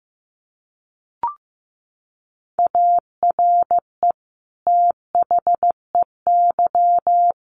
Morse code, heard as a keyed tone, about 15 wpm, 700 Hz. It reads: ARE THEY
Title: Sentences from Top 1400 Words 15wpm